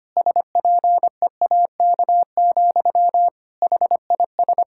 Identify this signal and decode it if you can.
Morse code, heard as a keyed tone, about 25 wpm, 700 Hz. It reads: SPEAK, HIS
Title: Sentences from Top 1600 Words 25wpm